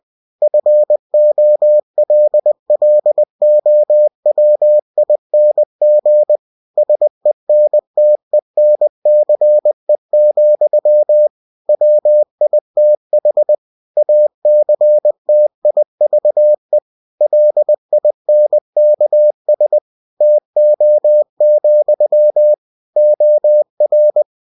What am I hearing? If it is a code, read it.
Morse code, heard as a keyed tone, about 20 wpm, 600 Hz. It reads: FOLLOWING SENTENCE, WITH ACTIVE LINKS TO, OR
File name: cw-books / winnie-the-pooh / winnie-20wpm-page-0114